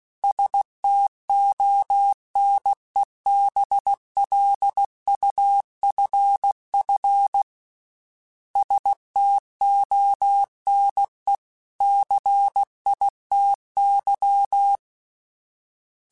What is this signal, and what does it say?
Morse code, heard as a keyed tone, about 16 wpm, 800 Hz. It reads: STONE BLUFF STONE CITY
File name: fromSMs / FromSMs mp3s / Cities@15